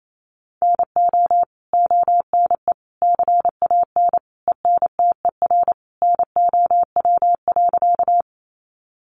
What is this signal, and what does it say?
Morse code, heard as a keyed tone, about 28 wpm, 700 Hz. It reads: NO ONE CAN ENTER NOW.
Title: Sentences from Top 1300 Words 28wpm